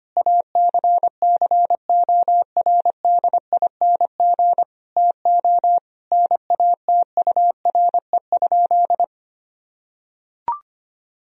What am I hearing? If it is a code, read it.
Morse code, heard as a keyed tone, about 25 wpm, 700 Hz. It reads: ACCORDING TO NATURE?
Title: Sentences from Top 1000 Words - Review Part 4 - 25wpm